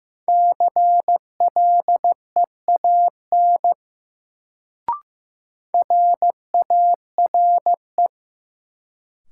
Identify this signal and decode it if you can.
Morse code, heard as a keyed tone, about 15 wpm, 700 Hz. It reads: CLEAN RARE
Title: Sentences from Top 1100 Words - New Word Drill - 15wpm